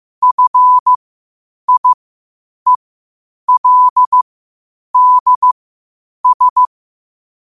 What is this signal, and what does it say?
Morse code, heard as a keyed tone, about 15 wpm, spaced as for 8 wpm, 1000 Hz. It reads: FIELDS
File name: qotdfiles / 8wpm